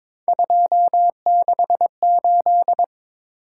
Morse code, heard as a keyed tone, about 22 wpm, 700 Hz. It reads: 268
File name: practice / 22wpm / Three Letter-Number Combinations - 22wpm